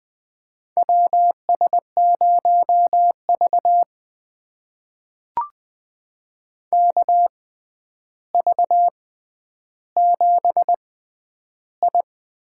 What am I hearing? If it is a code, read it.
Morse code, heard as a keyed tone, about 20 wpm, spaced as for 10 wpm, 700 Hz. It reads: WS0V KV7I
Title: USA Call Signs - 4 Characters - 6x Character Spacing - 20wpm